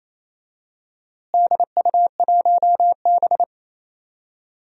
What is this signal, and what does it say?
Morse code, heard as a keyed tone, about 28 wpm, 700 Hz. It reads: DU1B
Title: DXCC-100 Call Signs - 4 Characters - Repeated 3x - 28wpm